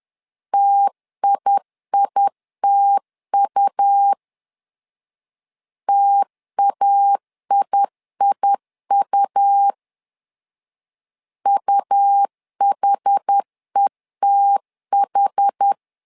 Morse code, heard as a keyed tone, about 10 wpm, 800 Hz. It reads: TIITU TAIIU UHETH
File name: LEZIONI / 2 UHDN / 2 UH 10wpm 1714